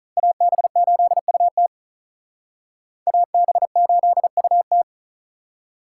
Morse code, heard as a keyed tone, about 44 wpm, 700 Hz. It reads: AB8UT AB8UT